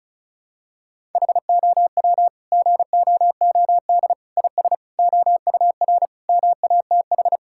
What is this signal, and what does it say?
Morse code, heard as a keyed tone, about 36 wpm, 700 Hz. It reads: HOW GOOD IS OUR MATH
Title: Sentences from Top 1000 Words - Review Part 6 - 35wpm